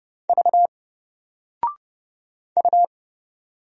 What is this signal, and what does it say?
Morse code, heard as a keyed tone, about 31 wpm, 700 Hz. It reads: V U
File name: Single-Letter-Number-30wpm